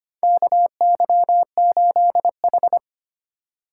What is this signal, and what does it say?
Morse code, heard as a keyed tone, about 25 wpm, 700 Hz. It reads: KY8H
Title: USA Call Signs - 4 Characters - 2x Character Spacing - 25wpm